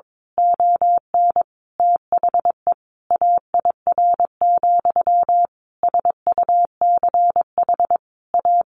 Morse code, heard as a keyed tone, about 22 wpm, 700 Hz. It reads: ON THE AIR, SUCH A
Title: A Sound Of Thunder 22wpm